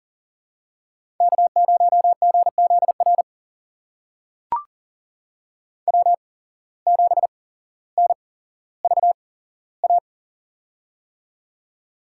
Morse code, heard as a keyed tone, about 37 wpm, spaced as for 17 wpm, 700 Hz. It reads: K0GZR W7NVA
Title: USA Call Signs - 5 Characters - 8x Character Spacing - 40wpm